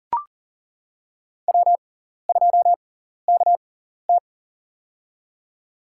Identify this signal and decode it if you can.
Morse code, heard as a keyed tone, about 39 wpm, spaced as for 14 wpm, 700 Hz. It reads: W2KT
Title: USA Call Signs - 4 Characters - 6x Character Spacing - 40wpm